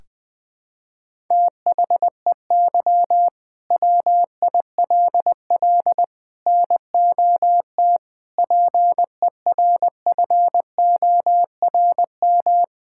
Morse code, heard as a keyed tone, about 20 wpm, 700 Hz. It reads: THEY WILL NOT PERFORM
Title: Sentences from Top 1000 Words - Review Part 5 - 20wpm